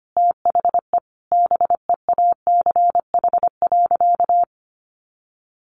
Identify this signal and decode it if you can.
Morse code, heard as a keyed tone, about 25 wpm, 700 Hz. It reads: THE BEACH.
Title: Sentences from Top 1500 Words 25wpm